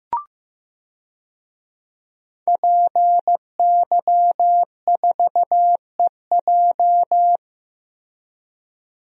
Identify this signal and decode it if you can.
Morse code, heard as a keyed tone, about 15 wpm, 700 Hz. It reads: PY4EJ
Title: DXCC-100 Call Signs - 5 Characters - Repeated 2x - 15wpm